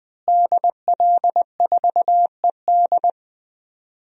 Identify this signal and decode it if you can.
Morse code, heard as a keyed tone, about 20 wpm, 700 Hz. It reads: DL4ED
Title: DXCC-100 Call Signs - 5 Characters - Repeated 2x - 20wpm